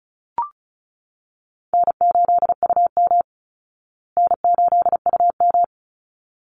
Morse code, heard as a keyed tone, about 34 wpm, 700 Hz. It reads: N8UM N8UM
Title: Easy Call Signs Repeated 2x - 35wpm